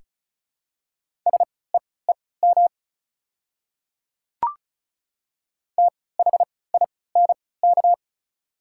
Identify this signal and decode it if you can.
Morse code, heard as a keyed tone, about 36 wpm, spaced as for 20 wpm, 690 Hz. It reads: SEEM THINK
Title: 3x Character Spacing - Top 300 Words - 35wpm